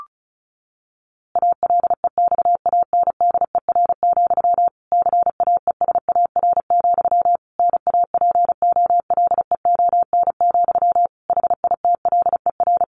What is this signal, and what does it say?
Morse code, heard as a keyed tone, about 34 wpm, 700 Hz. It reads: ALEXANDER, CAESAR, NAPOLEON, HITLER
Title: A Sound Of Thunder 35wpm